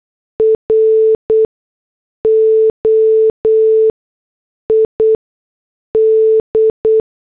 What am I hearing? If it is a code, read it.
Morse code, heard as a keyed tone, about 8 wpm, spaced as for 6 wpm, 440 Hz. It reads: ROID